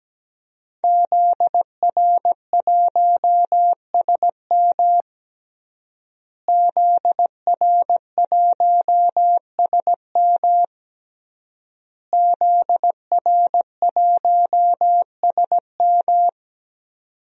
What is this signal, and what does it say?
Morse code, heard as a keyed tone, about 17 wpm, 700 Hz. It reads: ZR1SM ZR1SM ZR1SM
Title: DXCC-100 Call Signs - 5 Characters - Repeated 3x - 17wpm